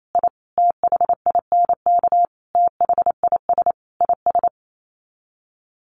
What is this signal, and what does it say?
Morse code, heard as a keyed tone, about 28 wpm, 700 Hz. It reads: I THINK THIS IS
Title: Sets of 5 Words Encore 28wpm